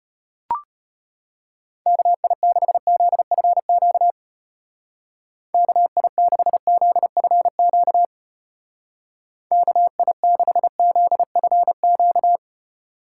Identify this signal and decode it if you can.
Morse code, heard as a keyed tone, about 33 wpm, 700 Hz. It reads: KI6ZFQ KI6ZFQ KI6ZFQ